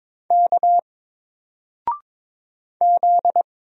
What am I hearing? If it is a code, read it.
Morse code, heard as a keyed tone, about 22 wpm, 700 Hz. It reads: K Z
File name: Single-Letter-Number-22wpm